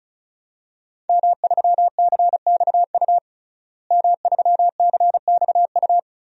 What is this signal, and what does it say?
Morse code, heard as a keyed tone, about 36 wpm, 700 Hz. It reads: M3CXU M3CXU
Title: DXCC-100 Call Signs - 5 Characters - Repeated 3x - 35wpm